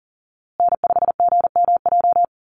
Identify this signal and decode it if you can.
Morse code, heard as a keyed tone, about 40 wpm, 700 Hz. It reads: N5GMJ
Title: USA Call Signs Repeated 2x 40wpm